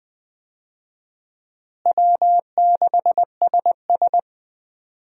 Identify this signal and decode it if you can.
Morse code, heard as a keyed tone, about 20 wpm, 700 Hz. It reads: W6SS
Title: USA Call Signs - 4 Characters - Repeated 2x - 20wpm